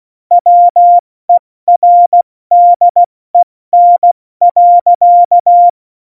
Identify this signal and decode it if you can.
Morse code, heard as a keyed tone, about 16 wpm, 700 Hz. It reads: WERDEN.
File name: Morsekurs / Witze / Witz 12. Januar 2021, 70 F80